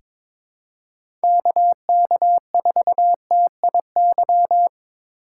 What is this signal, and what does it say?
Morse code, heard as a keyed tone, about 22 wpm, 700 Hz. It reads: KK4TIY
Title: USA Call Signs - 6 Characters - 4x Character Spacing - 22wpm